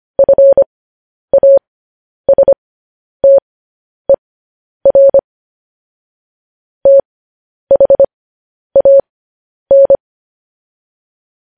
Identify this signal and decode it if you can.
Morse code, heard as a keyed tone, about 25 wpm, spaced as for 10 wpm, 550 Hz. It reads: FASTER THAN
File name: podcast / News headlines-1x-25x10